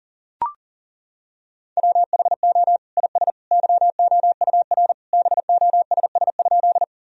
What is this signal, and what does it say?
Morse code, heard as a keyed tone, about 40 wpm, 700 Hz. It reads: WHO IS YOUR BOSS?
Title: Sentences from Top 1000 Words - Review Part 4 - 40wpm